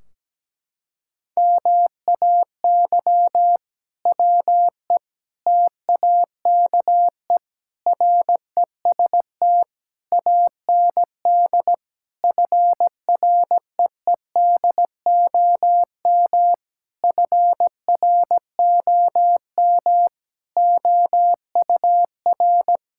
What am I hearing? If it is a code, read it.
Morse code, heard as a keyed tone, about 17 wpm, 700 Hz. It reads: MAY WE TAKE REST AND FREEDOM FROM OUR
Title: Sentences from Top 1600 Words 17wpm